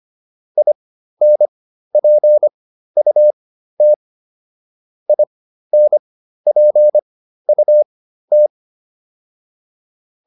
Morse code, heard as a keyed tone, about 25 wpm, spaced as for 13 wpm, 600 Hz. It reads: INPUT INPUT